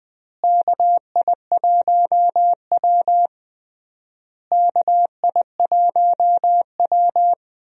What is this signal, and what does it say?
Morse code, heard as a keyed tone, about 19 wpm, 700 Hz. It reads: KI1W KI1W
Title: USA Call Signs - 4 Characters - Repeated 3x - 20wpm